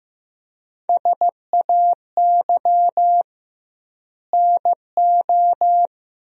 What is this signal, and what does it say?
Morse code, heard as a keyed tone, about 15 wpm, 700 Hz. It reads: SAY NO
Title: 2x Word-Spacing - Sentences from Top 200 Words - 15wpm